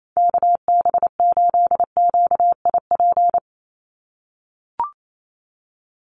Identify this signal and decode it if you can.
Morse code, heard as a keyed tone, about 28 wpm, 700 Hz. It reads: KB8QIP
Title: USA Call Signs 28wpm